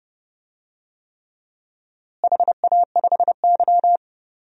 Morse code, heard as a keyed tone, about 30 wpm, 700 Hz. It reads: HA5Y
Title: DXCC-100 Call Signs - 4 Characters - Repeated 3x - 30wpm